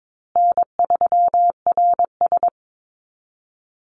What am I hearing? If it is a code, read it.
Morse code, heard as a keyed tone, about 22 wpm, 700 Hz. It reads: N3RS
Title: Easy Call Signs Repeated 2x - 22wpm